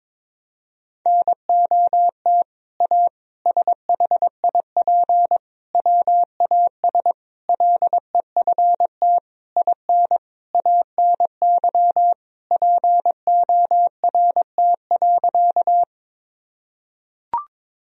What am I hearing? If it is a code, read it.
Morse code, heard as a keyed tone, about 22 wpm, 700 Hz. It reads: NOT A SHIP WAS LEFT IN ANY PORT.
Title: Sentences from Top 1000 Words - Review Part 2 - 22wpm